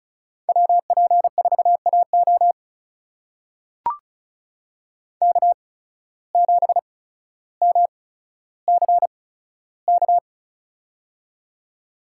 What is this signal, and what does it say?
Morse code, heard as a keyed tone, about 34 wpm, spaced as for 15 wpm, 690 Hz. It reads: WP4AO K7MCK
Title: USA Call Signs - 5 Characters - 8x Character Spacing - 35wpm